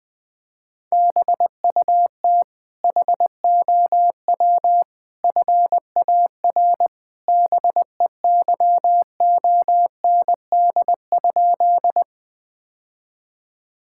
Morse code, heard as a keyed tone, about 20 wpm, 700 Hz. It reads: BUT HOW FAR BEYOND?
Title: Sentences from Top 1000 Words - Review Part 6 - 20wpm